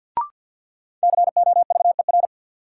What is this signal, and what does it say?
Morse code, heard as a keyed tone, about 51 wpm, 700 Hz. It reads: COVER
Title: Sentences from Top 500 Words - New Word Drill - 50wpm